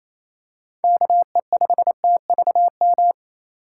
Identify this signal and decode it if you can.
Morse code, heard as a keyed tone, about 27 wpm, 700 Hz. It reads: KE5TVM
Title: USA Call Signs - 6 Characters - 4x Character Spacing - 28wpm